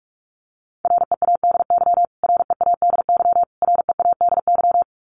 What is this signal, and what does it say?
Morse code, heard as a keyed tone, about 44 wpm, 700 Hz. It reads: READY READY READY
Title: Top 500 Words Repeated 45wpm